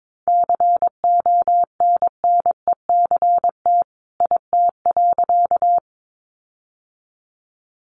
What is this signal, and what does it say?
Morse code, heard as a keyed tone, about 22 wpm, 700 Hz. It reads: CONNECT IT.
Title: Sentences from Top 1500 Words 22wpm